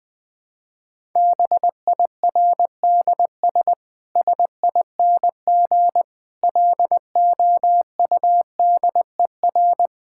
Morse code, heard as a keyed tone, about 20 wpm, 700 Hz. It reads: BIRDS SING LOUDER
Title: Difficult Sentences with Hints - 20wpm